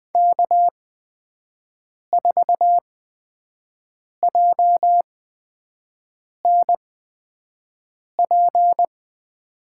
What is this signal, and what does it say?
Morse code, heard as a keyed tone, about 21 wpm, spaced as for 5 wpm, 700 Hz. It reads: K4JNP
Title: USA Call Signs - 5 Characters - 8x Character Spacing - 20wpm